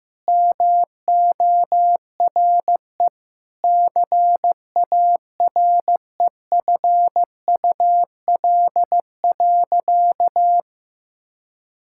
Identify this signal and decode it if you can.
Morse code, heard as a keyed tone, about 15 wpm, 700 Hz. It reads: MORE CAREFUL.
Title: Sentences from Top 1600 Words 15wpm